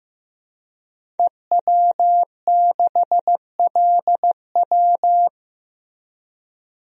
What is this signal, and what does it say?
Morse code, heard as a keyed tone, about 15 wpm, 700 Hz. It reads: EW6LW
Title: DXCC-100 Call Signs - 5 Characters - 15wpm